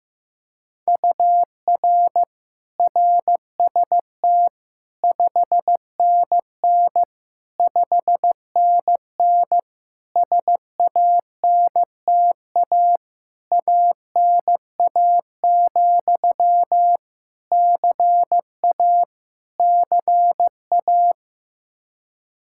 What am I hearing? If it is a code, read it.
Morse code, heard as a keyed tone, about 16 wpm, 700 Hz. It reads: UR RST 5NN 5NN SANTA ANA, CA CA
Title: Practice QSO - Exchange 1 - Beginning - 15wpm